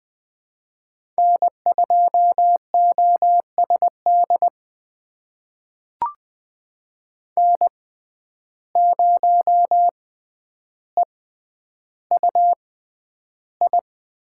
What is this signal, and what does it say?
Morse code, heard as a keyed tone, about 20 wpm, spaced as for 10 wpm, 700 Hz. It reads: N2OSD N0EUI
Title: USA Call Signs - 5 Characters - 6x Character Spacing - 20wpm